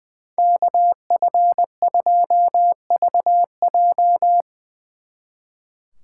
Morse code, heard as a keyed tone, about 22 wpm, 700 Hz. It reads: KF2VJ
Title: Speed Racing - USA 2x2 Call Signs 20wpm